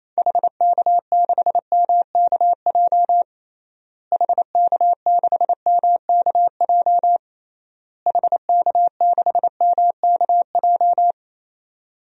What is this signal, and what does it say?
Morse code, heard as a keyed tone, about 28 wpm, 700 Hz. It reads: HK6MKJ HK6MKJ HK6MKJ
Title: DXCC-100 Call Signs - 6 Characters - Repeated 3x - 28wpm